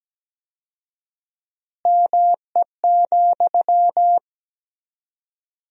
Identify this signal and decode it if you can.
Morse code, heard as a keyed tone, about 17 wpm, 700 Hz. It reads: ME,
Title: 4x Word-Spacing - Sentences from Top 200 Words - 17wpm